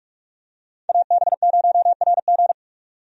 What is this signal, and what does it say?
Morse code, heard as a keyed tone, about 46 wpm, 700 Hz. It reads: AB0RG
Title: USA Call Signs - 5 Characters - 45wpm